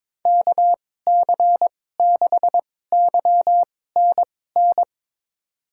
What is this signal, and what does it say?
Morse code, heard as a keyed tone, about 22 wpm, spaced as for 16 wpm, 700 Hz. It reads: KC6YNN
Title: USA Call Signs - 6 Characters - 2x Character Spacing - 22wpm